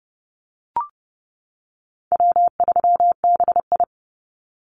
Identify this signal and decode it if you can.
Morse code, heard as a keyed tone, about 30 wpm, 700 Hz. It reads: W3BI
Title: Easy Call Signs - 30wpm